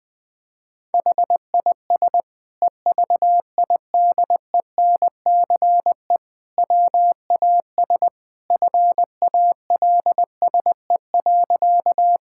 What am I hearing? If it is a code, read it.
Morse code, heard as a keyed tone, about 20 wpm, 700 Hz. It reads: HIS EVIDENCE WAS FALSE.
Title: Sentences from Top 1600 Words 20wpm